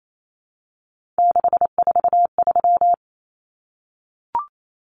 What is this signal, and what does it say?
Morse code, heard as a keyed tone, about 28 wpm, 700 Hz. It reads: 643
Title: Common Cut Numbers - 3-Digit 28wpm